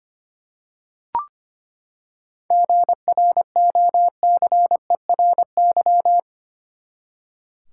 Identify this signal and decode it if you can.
Morse code, heard as a keyed tone, about 25 wpm, 700 Hz. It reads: GROCERY
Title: Seven Letter Words - 25wpm